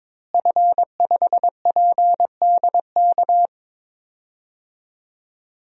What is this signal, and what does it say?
Morse code, heard as a keyed tone, about 22 wpm, 700 Hz. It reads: F5PDK
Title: DXCC-100 Call Signs - 5 Characters - Repeated 3x - 22wpm